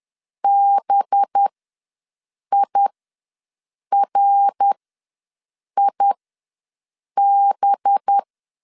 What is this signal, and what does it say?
Morse code, heard as a keyed tone, about 10 wpm, spaced as for 6 wpm, 800 Hz. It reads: BIRIB
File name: LEZIONI / 5 LBVC / 5 LB ws 10wpm 3616